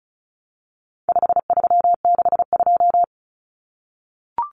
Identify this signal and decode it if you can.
Morse code, heard as a keyed tone, about 34 wpm, 700 Hz. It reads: 5362
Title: Common Cut Numbers - 4-Digit 35wpm